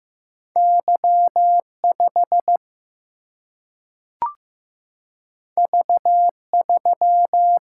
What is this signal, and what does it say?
Morse code, heard as a keyed tone, about 15 wpm, 700 Hz. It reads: Y5 V3
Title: Single-Letter Single-Number 15wpm